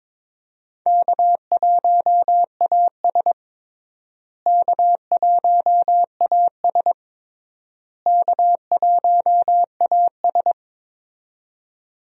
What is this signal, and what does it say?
Morse code, heard as a keyed tone, about 22 wpm, 700 Hz. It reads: K1AS K1AS K1AS
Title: USA Call Signs - 4 Characters - Repeated 3x - 22wpm